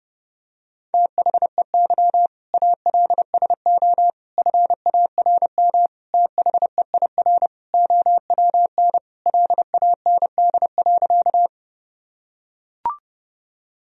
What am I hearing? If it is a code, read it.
Morse code, heard as a keyed tone, about 31 wpm, 700 Hz. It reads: THEY ALSO FARM THEIR OWN LAND.
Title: Sentences from Top 1000 Words - Review Part 2 - 30wpm